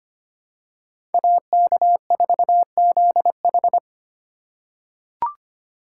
Morse code, heard as a keyed tone, about 25 wpm, 700 Hz. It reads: AK4ZH
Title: USA Call Signs - 5 Characters - 3x Character Spacing - 25wpm